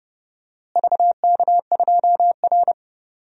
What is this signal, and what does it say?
Morse code, heard as a keyed tone, about 30 wpm, 700 Hz. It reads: VK2R
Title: DXCC-100 Call Signs - 4 Characters - Repeated 3x - 30wpm